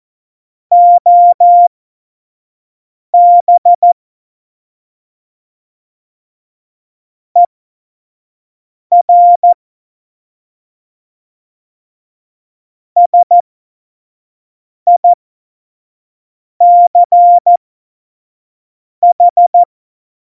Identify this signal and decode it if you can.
Morse code, heard as a keyed tone, about 14 wpm, spaced as for 5 wpm, 700 Hz. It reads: OB ER SICH
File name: Morsekurs / Witze / Witz 26 Mai 2020, 25 F70